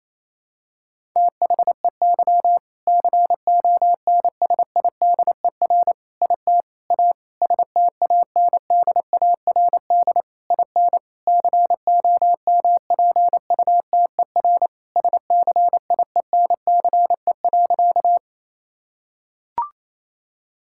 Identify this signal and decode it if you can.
Morse code, heard as a keyed tone, about 28 wpm, 700 Hz. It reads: THEY CONSIDER IT A STANDARD IN COMPUTER SCIENCE.
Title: Sentences from Top 1000 Words - Review Part 6 - 28wpm